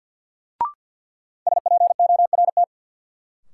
Morse code, heard as a keyed tone, about 50 wpm, 700 Hz. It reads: SPORT
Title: Sentences from Top 1000 Words - Review Part 4 - New Word Drill - 50wpm